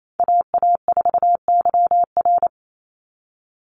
Morse code, heard as a keyed tone, about 28 wpm, 700 Hz. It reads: AA4YR
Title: USA Call Signs 28wpm